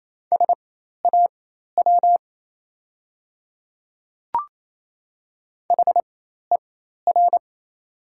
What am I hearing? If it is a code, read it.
Morse code, heard as a keyed tone, about 28 wpm, spaced as for 13 wpm, 700 Hz. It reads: SAW HER
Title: 4x Character Spacing - Top 500 Words - 28wpm